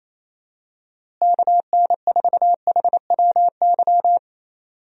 Morse code, heard as a keyed tone, about 28 wpm, 700 Hz. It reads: KN4HWY
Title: USA Call Signs - 6 Characters - Repeated 3x - 28wpm